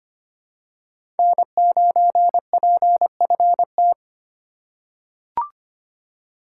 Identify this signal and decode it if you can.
Morse code, heard as a keyed tone, about 25 wpm, 700 Hz. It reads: N9PFT
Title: USA Call Signs - 3x Character Spacing - 25wpm